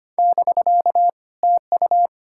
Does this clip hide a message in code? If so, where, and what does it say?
Yes: Morse code from 0.2 to 2.1 s, heard as a keyed tone, about 33 wpm, 700 Hz. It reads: <BK> TU